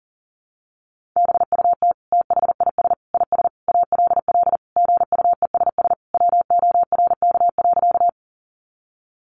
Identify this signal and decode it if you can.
Morse code, heard as a keyed tone, about 40 wpm, 700 Hz. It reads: BUT THIS IS ALL GUESS WORK.
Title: Sentences from Top 1500 Words 40wpm